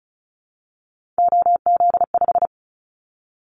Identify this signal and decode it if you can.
Morse code, heard as a keyed tone, about 36 wpm, 700 Hz. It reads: OZ5
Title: Callsign Prefixes for Top 100 DXCC - 35wpm